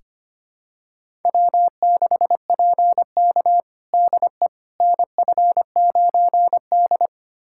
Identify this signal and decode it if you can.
Morse code, heard as a keyed tone, about 25 wpm, 700 Hz. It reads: W6PK DE NF9D
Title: USA Callsign DE Callsign - 4 Characters - 25wpm